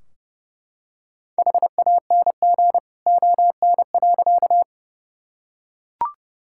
Morse code, heard as a keyed tone, about 30 wpm, 700 Hz. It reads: HANG ON.
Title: Sentences from Top 1600 Words 30wpm